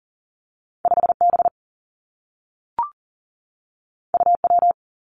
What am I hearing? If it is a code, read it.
Morse code, heard as a keyed tone, about 40 wpm, 700 Hz. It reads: 5 B U W
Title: Two Challenging Letter-Number Combinations 40wpm